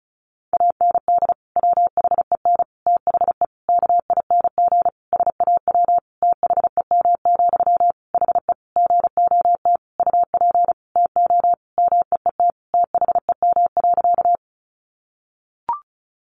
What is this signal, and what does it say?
Morse code, heard as a keyed tone, about 34 wpm, 700 Hz. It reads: AND WHEN THE KING SAW THEM, HE GOT UP TO MEET THEM.
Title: Sentences from Top 1100 Words 35wpm